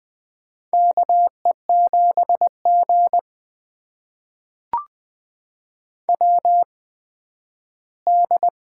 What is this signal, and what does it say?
Morse code, heard as a keyed tone, about 20 wpm, spaced as for 12 wpm, 700 Hz. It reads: KE7G WD
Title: USA Call Signs - 4 Characters - 8x Character Spacing - 20wpm